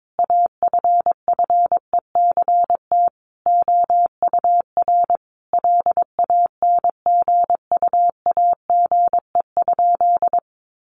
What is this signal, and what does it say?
Morse code, heard as a keyed tone, about 22 wpm, 700 Hz. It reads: AFFECT OUR LANGUAGE?
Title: Sentences from Top 900 Words 22wpm